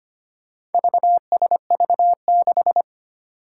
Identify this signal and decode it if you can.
Morse code, heard as a keyed tone, about 25 wpm, 700 Hz. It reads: VSV6
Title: Four Challenging Letter-Number Combinations 25wpm